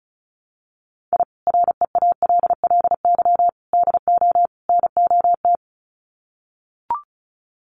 Morse code, heard as a keyed tone, about 34 wpm, 700 Hz. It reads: I REALLY DO NOT
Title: Sets of 4 Words 35wpm